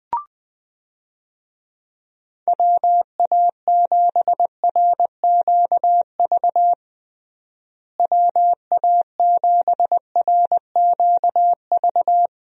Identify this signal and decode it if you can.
Morse code, heard as a keyed tone, about 20 wpm, 700 Hz. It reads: WA7RQV WA7RQV
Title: USA Call Signs - 6 Characters - Repeated 3x - 20wpm